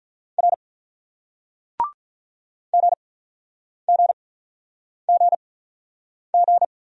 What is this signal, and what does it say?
Morse code, heard as a keyed tone, about 48 wpm, spaced as for 9 wpm, 700 Hz. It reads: R GGGG